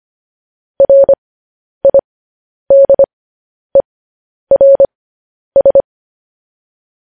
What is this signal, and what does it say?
Morse code, heard as a keyed tone, about 25 wpm, spaced as for 10 wpm, 550 Hz. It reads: RIDERS
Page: News Headlines in Morse Code at 10 WPM